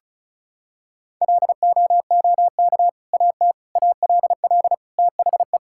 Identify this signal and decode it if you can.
Morse code, heard as a keyed tone, about 36 wpm, 700 Hz. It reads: LOOK AT ALL THE
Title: Sentences from Top 1000 Words - Review Part 5 - 35wpm